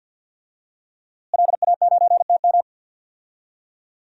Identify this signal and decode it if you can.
Morse code, heard as a keyed tone, about 49 wpm, 700 Hz. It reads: LA9TM
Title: DXCC-100 Call Signs - 5 Characters - Repeated 2x - 50wpm